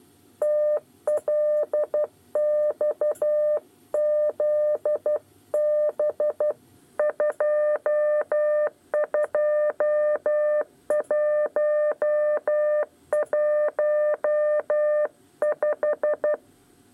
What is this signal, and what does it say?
Morse code, heard as a keyed tone, about 12 wpm, 600 Hz. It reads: TLXZB 22115
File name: 8 bis lezione cw 2